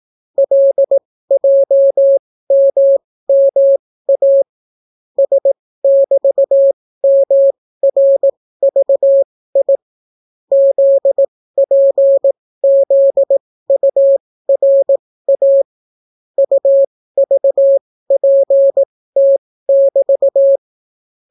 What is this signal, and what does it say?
Morse code, heard as a keyed tone, about 18 wpm, spaced as for 15 wpm, 550 Hz. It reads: LJMMA S<BT>MRVI ZPZURA UVPT<BT>